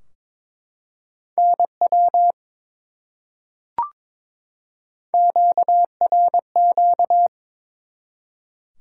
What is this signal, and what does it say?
Morse code, heard as a keyed tone, about 22 wpm, 700 Hz. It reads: NW QRQ
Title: Top 180 QSO Elements 22wpm